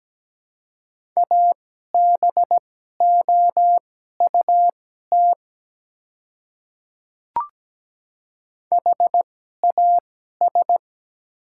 Morse code, heard as a keyed tone, about 17 wpm, spaced as for 12 wpm, 700 Hz. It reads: ABOUT HAS
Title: 2x Character Spacing - Top 100 Words - 17wpm